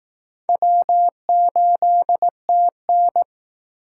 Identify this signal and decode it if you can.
Morse code, heard as a keyed tone, about 20 wpm, 700 Hz. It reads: W8TN